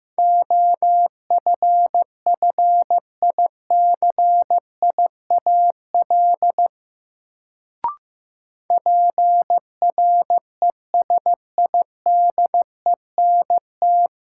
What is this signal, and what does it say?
Morse code, heard as a keyed tone, about 15 wpm, 700 Hz. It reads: OFFICIAL PRESIDENT
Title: Sentences from Top 1000 Words - New Word Drill - 15wpm